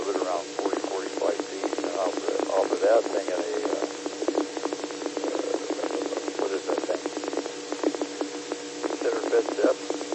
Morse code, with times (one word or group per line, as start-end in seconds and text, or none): none